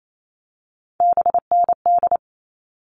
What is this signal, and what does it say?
Morse code, heard as a keyed tone, about 28 wpm, 700 Hz. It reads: BND
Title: All Cut Numbers - 3-Digit 28wpm